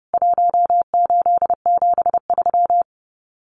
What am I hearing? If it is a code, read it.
Morse code, heard as a keyed tone, about 30 wpm, 700 Hz. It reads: 1873
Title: CWT Exchange 30wpm